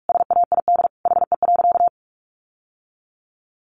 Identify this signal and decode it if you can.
Morse code, heard as a keyed tone, about 44 wpm, 700 Hz. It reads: SAID HE.
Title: Sentences from Top 1100 Words 45wpm